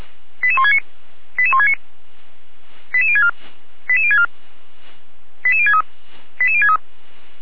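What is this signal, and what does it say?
Five-tone selective call (ZVEI), heard as quick runs of five tones, tones of about 80 ms. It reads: 88169 88169 88064 88064 88063 88063
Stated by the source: ZVEI five-tone selective call received by SDR.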